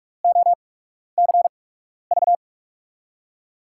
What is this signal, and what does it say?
Morse code, heard as a keyed tone, about 44 wpm, spaced as for 12 wpm, 700 Hz. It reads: OCV
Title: USA Call Signs - 6 Characters - 8x Character Spacing - 45wpm